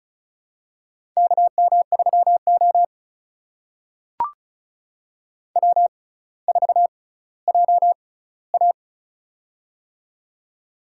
Morse code, heard as a keyed tone, about 36 wpm, spaced as for 18 wpm, 700 Hz. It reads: KM3O W4JA
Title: USA Call Signs - 4 Characters - 6x Character Spacing - 35wpm